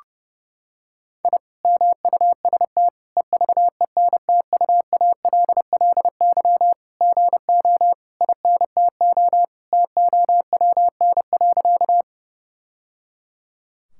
Morse code, heard as a keyed tone, about 32 wpm, 700 Hz. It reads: I MUST EVENTUALLY GO INTO TOWN.
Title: Sentences from Top 1000 Words - Review Part 6 - 30wpm